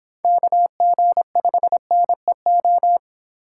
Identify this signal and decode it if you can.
Morse code, heard as a keyed tone, about 26 wpm, 700 Hz. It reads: KG5NEO